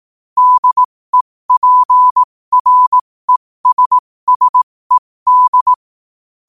Morse code, heard as a keyed tone, about 18 wpm, spaced as for 15 wpm, 1000 Hz. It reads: DEPRESSED
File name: qotdfiles / 15wpm